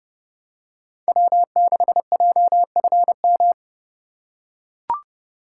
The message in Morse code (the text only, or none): W6JFM